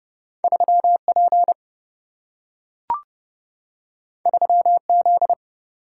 Morse code, heard as a keyed tone, about 29 wpm, 700 Hz. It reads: 3P 3Z
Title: Single-Letter Single-Number 30wpm